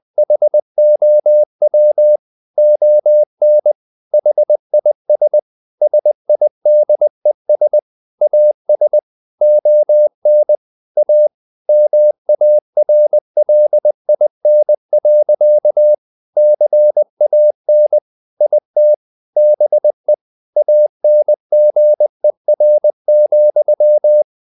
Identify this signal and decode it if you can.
Morse code, heard as a keyed tone, about 20 wpm, 600 Hz. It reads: HOW ON HIS SIDES AS ON A MARLIN. CAN IT BE ANGER,
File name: the old man and the sea part 2 20wpm